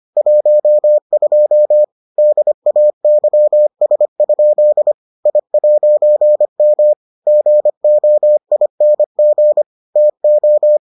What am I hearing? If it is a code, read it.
Morse code, heard as a keyed tone, about 25 wpm, 600 Hz. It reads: 12 DAYS? I'M GOING TO